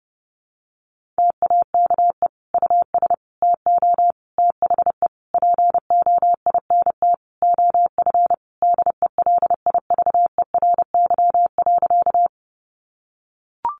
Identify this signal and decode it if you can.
Morse code, heard as a keyed tone, about 30 wpm, 700 Hz. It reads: TAKE US TO THE POINT OF DELIVERY.
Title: Sentences from Top 1500 Words 30wpm